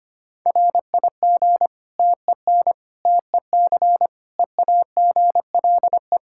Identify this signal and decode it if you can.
Morse code, heard as a keyed tone, about 25 wpm, 700 Hz. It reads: RIG TEN TEC EAGLE
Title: Practice QSO - Exchange 2 - Info - 25wpm